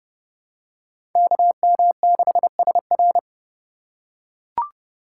Morse code, heard as a keyed tone, about 30 wpm, 700 Hz. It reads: KM6SR
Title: USA Call Signs - 5 Characters - 4x Character Spacing - 30wpm